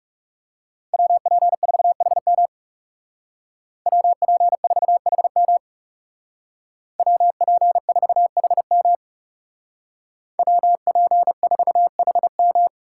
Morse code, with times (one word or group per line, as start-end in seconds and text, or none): 0.9-2.5 WP4HM
3.9-5.6 WP4HM
7.0-8.9 WP4HM
10.4-12.7 WP4HM